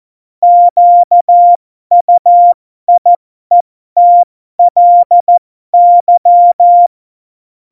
Morse code, heard as a keyed tone, about 14 wpm, spaced as for 12 wpm, 700 Hz. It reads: QUIETLY
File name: Morsekurs / Witze